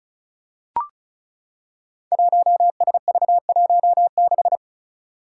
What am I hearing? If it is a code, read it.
Morse code, heard as a keyed tone, about 36 wpm, 700 Hz. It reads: 1SV16
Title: Five Challenging Letter-Number Combinations 35wpm